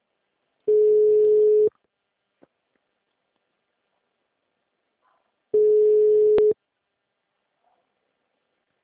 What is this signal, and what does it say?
Ringback tone (Europe, 425 Hz).